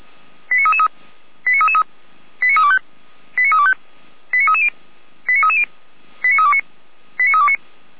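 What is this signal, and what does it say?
Five-tone selective call (ZVEI), heard as quick runs of five tones, tones of about 70 ms. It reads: 89333 89325 89325 89330 89330 89329 89329
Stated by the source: Five-tone ZVEI selective call received on SDR.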